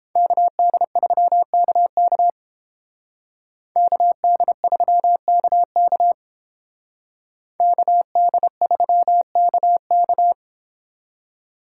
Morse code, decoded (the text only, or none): KD3KK KD3KK KD3KK